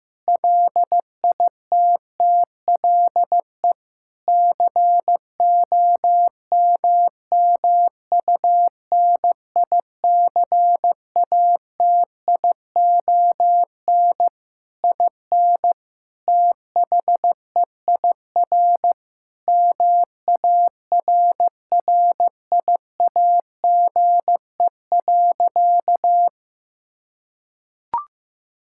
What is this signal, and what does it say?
Morse code, heard as a keyed tone, about 15 wpm, 700 Hz. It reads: LITTLE COMMUNICATION IN THEIR MARRIAGE.
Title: Sentences from Top 1000 Words - Review Part 6 - 15wpm